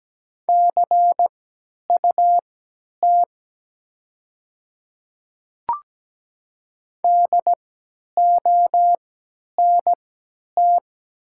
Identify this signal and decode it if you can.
Morse code, heard as a keyed tone, about 17 wpm, spaced as for 10 wpm, 700 Hz. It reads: CUT DONT
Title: 3x Character Spacing - Top 300 Words - 17wpm